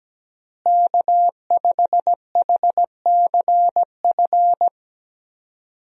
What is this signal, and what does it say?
Morse code, heard as a keyed tone, about 17 wpm, 700 Hz. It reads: K5HCF
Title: USA Call Signs - 5 Characters - Repeated 3x - 17wpm